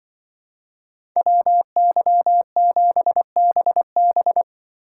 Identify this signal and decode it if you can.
Morse code, heard as a keyed tone, about 24 wpm, 700 Hz. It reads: WY7BB
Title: Speed Racing - USA 2x2 Call Signs 20wpm